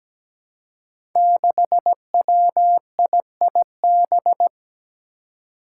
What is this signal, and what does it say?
Morse code, heard as a keyed tone, about 17 wpm, 700 Hz. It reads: 6WIIB
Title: Five Challenging Letter-Number Combinations 17wpm